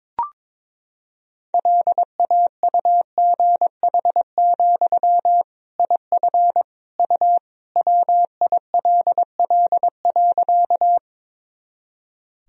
Morse code, heard as a keyed tone, about 22 wpm, 700 Hz. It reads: LAUGH, IF U WILL.
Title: Sentences from Top 500 Words and Abbreviations 22wpm